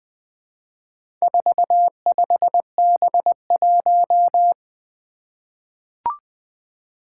Morse code, heard as a keyed tone, about 20 wpm, 700 Hz. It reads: 45B1
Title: Four Challenging Letter-Number Combinations 20wpm